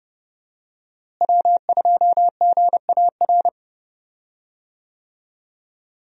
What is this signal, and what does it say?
Morse code, heard as a keyed tone, about 30 wpm, 700 Hz. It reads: W2GAR
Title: USA Call Signs - 5 Characters - Repeated 2x - 30wpm